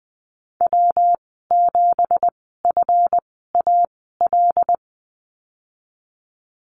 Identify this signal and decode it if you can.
Morse code, heard as a keyed tone, about 20 wpm, spaced as for 15 wpm, 700 Hz. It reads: W7FAL
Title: 193 Lesson 31g - Letter v - Call Signs 20wpm